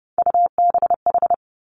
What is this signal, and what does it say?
Morse code, heard as a keyed tone, about 30 wpm, 700 Hz. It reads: UBH
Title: Three Challenging Letter-Number Combinations 30wpm